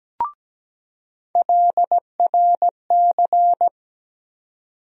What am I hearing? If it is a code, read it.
Morse code, heard as a keyed tone, about 17 wpm, 700 Hz. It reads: LRC
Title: USA Call Sign Postfixes - 3 Letters 17wpm